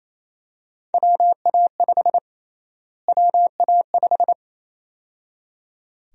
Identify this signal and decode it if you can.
Morse code, heard as a keyed tone, about 28 wpm, 700 Hz. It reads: WA5 WA5
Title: USA Call Sign Prefixes - Repeated 2x - 28wpm